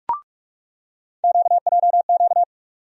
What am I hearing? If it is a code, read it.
Morse code, heard as a keyed tone, about 42 wpm, 700 Hz. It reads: QJQ
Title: USA Call Sign Postfixes - 3 Letters 45wpm